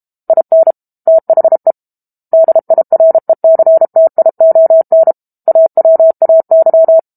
Morse code, heard as a keyed tone, about 32 wpm, 650 Hz. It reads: IN THE DIRECTION AWAY
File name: Book Two / 06-Chapter 6 - 32 WPM